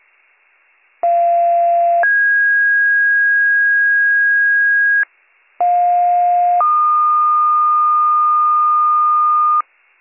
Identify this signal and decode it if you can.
Two-tone page, heard as a one-second tone, then a three-second tone, about 690 then 1650 Hz, about 710 then 1220 Hz, with noise. It is heard from 1.0 to 9.6 s.